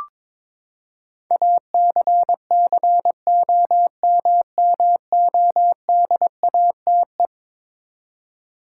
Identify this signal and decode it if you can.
Morse code, heard as a keyed tone, about 22 wpm, 700 Hz. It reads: ACCOMMODATE
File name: practice / 22wpm / Eleven Letter Words - 22wpm